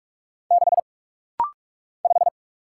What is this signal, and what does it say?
Morse code, heard as a keyed tone, about 45 wpm, 700 Hz. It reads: 6 5
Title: Single Letter-Number Alt - Rapid-Fire 45wpm